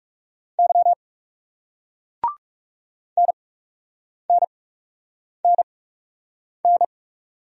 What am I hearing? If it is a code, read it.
Morse code, heard as a keyed tone, about 39 wpm, spaced as for 8 wpm, 700 Hz. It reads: Y NNNN